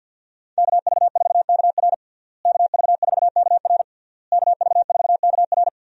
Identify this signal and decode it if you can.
Morse code, heard as a keyed tone, about 53 wpm, 700 Hz. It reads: KV4KR KV4KR KV4KR
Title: USA Call Signs - 5 Characters - Repeated 3x - 50wpm